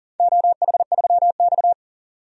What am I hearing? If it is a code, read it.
Morse code, heard as a keyed tone, about 38 wpm, 700 Hz. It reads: OH3X
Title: DXCC-100 Call Signs - 4 Characters - 40wpm